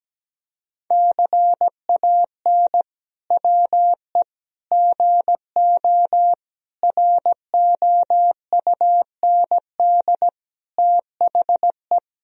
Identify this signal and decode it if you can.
Morse code, heard as a keyed tone, about 17 wpm, 700 Hz. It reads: CAN WE GO ROUND THE
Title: Sentences from Top 1000 Words - Review Part 1 - 17wpm